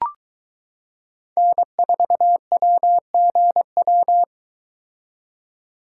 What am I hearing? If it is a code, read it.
Morse code, heard as a keyed tone, about 22 wpm, 700 Hz. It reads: N4WGW